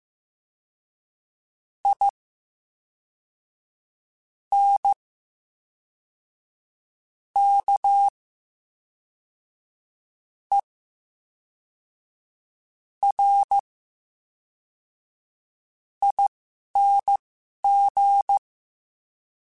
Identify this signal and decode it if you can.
Morse code, heard as a keyed tone, about 15 wpm, spaced as for 4 wpm, 800 Hz. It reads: INKERING
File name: Special INGs / Special INGs-mp3s / VerySlowToFaster Ing @10